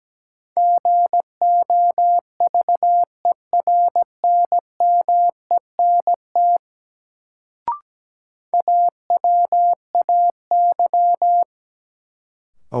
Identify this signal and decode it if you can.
Morse code, heard as a keyed tone, about 17 wpm, 700 Hz. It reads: GOVERNMENT AWAY
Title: Sentences from Top 600 Words - New Word Drill - 17wpm